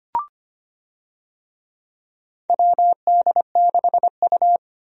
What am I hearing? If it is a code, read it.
Morse code, heard as a keyed tone, about 25 wpm, 700 Hz. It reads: WD6U
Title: USA Call Signs - 4 Characters - 25wpm